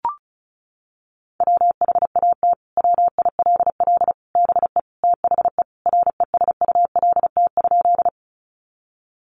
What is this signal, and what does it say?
Morse code, heard as a keyed tone, about 34 wpm, 700 Hz. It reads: WHAT WILL BE THE RESULT?
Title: Sentences from Top 700 Words 35wpm